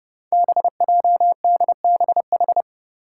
Morse code, heard as a keyed tone, about 32 wpm, 700 Hz. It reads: BJDBH
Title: Five Challenging Letter-Number Combinations 30wpm